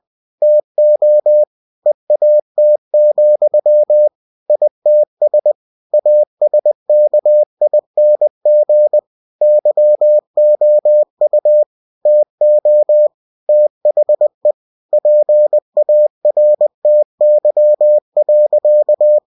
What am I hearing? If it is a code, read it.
Morse code, heard as a keyed tone, about 20 wpm, 600 Hz. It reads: TO EAT, ITS ASKING YOU TO THE PARTY.
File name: cw-books / winnie-the-pooh / winnie-20wpm-page-0105